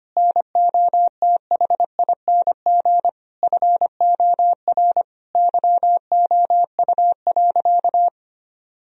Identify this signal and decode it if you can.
Morse code, heard as a keyed tone, about 25 wpm, 700 Hz. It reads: NOTHING FOR YOU.
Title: Sentences from Top 1000 Words - Review Part 3 - 25wpm